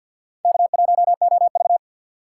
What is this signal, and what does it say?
Morse code, heard as a keyed tone, about 53 wpm, 700 Hz. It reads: K1OV
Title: USA Call Signs - 4 Characters - 6x Character Spacing - 50wpm